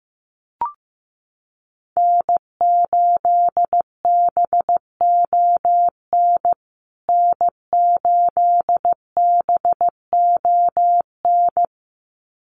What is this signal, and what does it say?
Morse code, heard as a keyed tone, about 15 wpm, 700 Hz. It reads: N8BON N8BON
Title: USA Call Signs Repeated 2x 15wpm